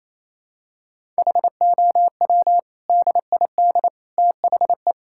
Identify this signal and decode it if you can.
Morse code, heard as a keyed tone, about 28 wpm, 700 Hz. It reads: HOW DID THE
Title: Sentences from Top 1000 Words - Review Part 6 - 28wpm